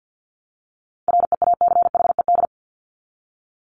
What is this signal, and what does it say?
Morse code, heard as a keyed tone, about 50 wpm, 700 Hz. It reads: REACHED
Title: ED Suffix - Top 100 Words with Base Words 50wpm